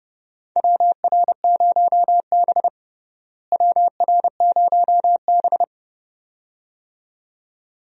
Morse code, heard as a keyed tone, about 30 wpm, 700 Hz. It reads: WR0B WR0B
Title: USA Call Signs - 4 Characters - Repeated 2x - 30wpm